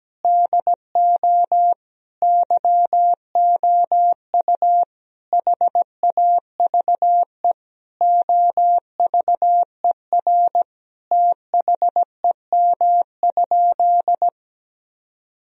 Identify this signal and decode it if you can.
Morse code, heard as a keyed tone, about 17 wpm, 700 Hz. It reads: DO YOU HAVE OVER THEM?
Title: Sentences from Top 1000 Words - Review Part 3 - 17wpm